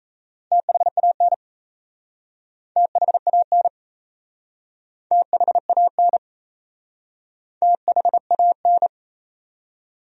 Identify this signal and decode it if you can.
Morse code, heard as a keyed tone, about 38 wpm, 700 Hz. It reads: THAN THAN THAN THAN